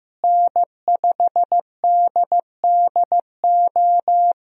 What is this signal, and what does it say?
Morse code, heard as a keyed tone, about 15 wpm, 700 Hz. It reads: N5DDO
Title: USA Call Signs - 5 Characters - Repeated 3x - 15wpm